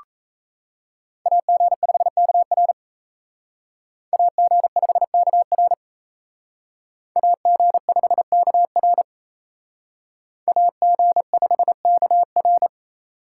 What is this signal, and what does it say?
Morse code, heard as a keyed tone, about 36 wpm, 700 Hz. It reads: AG5KR AG5KR AG5KR AG5KR